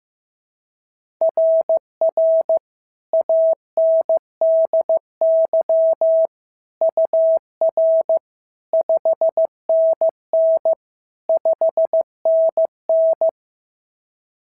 Morse code, heard as a keyed tone, about 14 wpm, 650 Hz. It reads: RR ANDY UR 5NN 5NN
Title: SOTA QSO Practice 15wpm